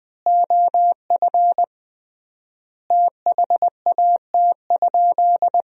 Morse code, heard as a keyed tone, about 20 wpm, 700 Hz. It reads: OF THAT?
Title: 3x Word-Spacing - Sentences from Top 100 Words - 20wpm